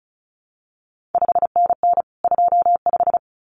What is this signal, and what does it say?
Morse code, heard as a keyed tone, about 36 wpm, 700 Hz. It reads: 5NN 25
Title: CQ WW DX Contest Exchange - 35wpm